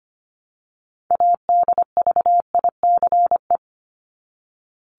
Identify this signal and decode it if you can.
Morse code, heard as a keyed tone, about 25 wpm, 700 Hz. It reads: ADVICE
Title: Top 1001-1250 Words 25wpm